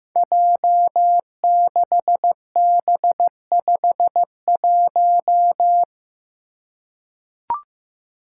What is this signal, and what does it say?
Morse code, heard as a keyed tone, about 16 wpm, 700 Hz. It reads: J6B51
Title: Five Challenging Letter-Number Combinations 15wpm